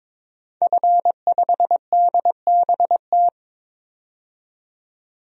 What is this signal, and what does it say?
Morse code, heard as a keyed tone, about 22 wpm, 700 Hz. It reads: F5DBT
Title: DXCC-100 Call Signs - 5 Characters - 22wpm